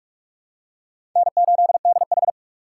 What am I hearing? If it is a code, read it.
Morse code, heard as a keyed tone, about 44 wpm, 700 Hz. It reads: N8DH
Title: USA Call Signs - 4 Characters - 4x Character Spacing - 45wpm